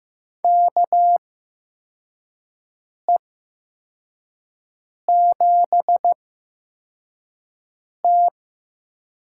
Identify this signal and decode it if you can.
Morse code, heard as a keyed tone, about 15 wpm, spaced as for 4 wpm, 700 Hz. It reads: KE7T
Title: USA Call Signs - 8x Character Spacing - 15wpm